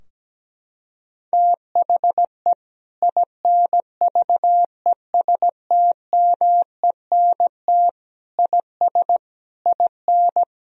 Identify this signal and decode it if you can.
Morse code, heard as a keyed tone, about 17 wpm, 700 Hz. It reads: THE INVESTMENT IS IN
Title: Sentences from Top 1000 Words - Review Part 4 - 17wpm